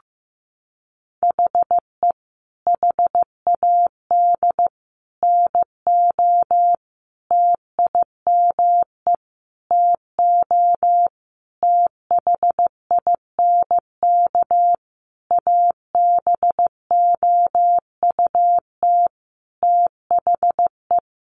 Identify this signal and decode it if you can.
Morse code, heard as a keyed tone, about 15 wpm, 700 Hz. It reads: HE HAD NO TIME TO THINK ABOUT THE
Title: Sentences from Top 700 Words 15wpm